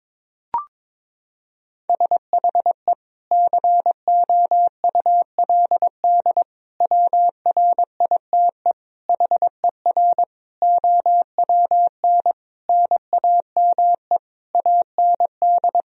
Morse code, heard as a keyed tone, about 22 wpm, 700 Hz. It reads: SHE COULD WRITE HER OWN NAME AND
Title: Sentences from Top 1000 Words - Review Part 1 - 22wpm